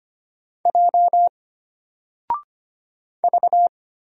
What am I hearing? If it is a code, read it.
Morse code, heard as a keyed tone, about 25 wpm, 700 Hz. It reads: J V
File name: Single-Letter-Number-25wpm